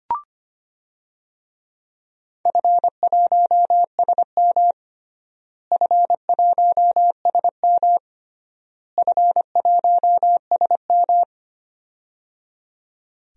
Morse code, heard as a keyed tone, about 25 wpm, 700 Hz. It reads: F1SM F1SM F1SM
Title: DXCC-100 Call Signs - 4 Characters - Repeated 3x - 25wpm